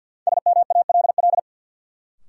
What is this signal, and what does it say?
Morse code, heard as a keyed tone, about 52 wpm, 700 Hz. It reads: SMALL
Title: Sets of 2 Words - New Word Drill - 50wpm